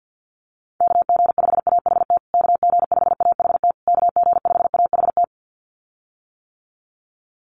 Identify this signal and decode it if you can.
Morse code, heard as a keyed tone, about 50 wpm, 700 Hz. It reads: KG5AHT KG5AHT KG5AHT
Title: USA Call Signs Repeated 3x 50wpm